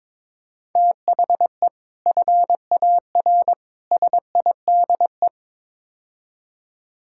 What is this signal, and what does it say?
Morse code, heard as a keyed tone, about 22 wpm, 700 Hz. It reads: THE FAR SIDE
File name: practice / 22wpm / Build A Phrase - 5-Word Phrases - 22wpm